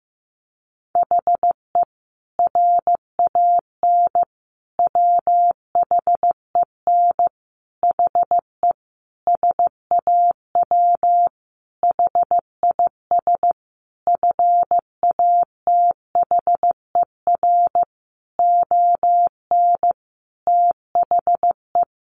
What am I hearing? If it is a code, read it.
Morse code, heard as a keyed tone, about 15 wpm, 700 Hz. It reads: HE RAN WHEN HE SAW HIS FATHER ON THE
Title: Sentences from Top 500 Words 15wpm